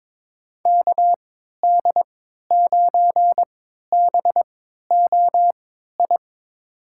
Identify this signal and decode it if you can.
Morse code, heard as a keyed tone, about 22 wpm, spaced as for 12 wpm, 700 Hz. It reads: KD9BOI
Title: USA Call Signs - 3x Character Spacing - 22wpm